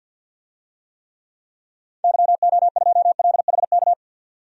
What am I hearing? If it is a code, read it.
Morse code, heard as a keyed tone, about 49 wpm, 700 Hz. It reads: YO2LHK
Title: DXCC-100 Call Signs - 6 Characters - 50wpm